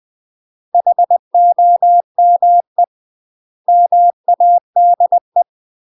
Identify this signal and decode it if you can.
Morse code, heard as a keyed tone, about 20 wpm, 700 Hz. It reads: HOME MADE
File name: UK MASTER QSOs TRAINING / 20WPM / UK MASTER QSOs 0058